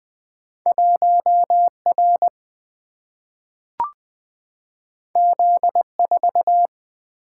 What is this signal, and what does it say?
Morse code, heard as a keyed tone, about 20 wpm, 700 Hz. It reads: 1R Z4
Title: Single-Letter Single-Number 20wpm